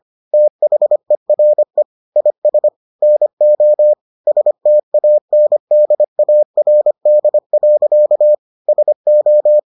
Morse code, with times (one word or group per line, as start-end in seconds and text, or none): 0.3-1.8 THERE
2.2-2.7 IS
3.0-3.9 NO
4.3-8.3 STANDARD.
8.7-9.6 SO